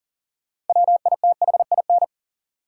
Morse code, heard as a keyed tone, about 40 wpm, 700 Hz. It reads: WITHIN
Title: Sets of 2 Words - New Word Drill - 40wpm